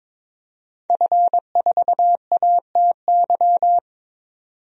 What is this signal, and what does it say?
Morse code, heard as a keyed tone, about 22 wpm, 700 Hz. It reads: F4ATY
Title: DXCC-100 Call Signs - 5 Characters - 22wpm